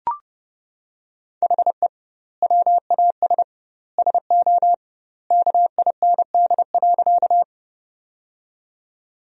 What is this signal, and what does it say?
Morse code, heard as a keyed tone, about 30 wpm, 700 Hz. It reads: HE WAS SO KIND.
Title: 2x Word-Spacing - Sentences from Top 200 Words - 30wpm